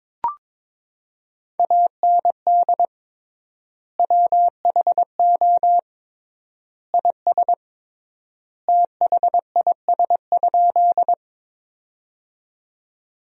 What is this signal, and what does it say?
Morse code, heard as a keyed tone, about 21 wpm, 700 Hz. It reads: AND WHO IS THIS?
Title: 3x Word-Spacing - Sentences from Top 100 Words - 22wpm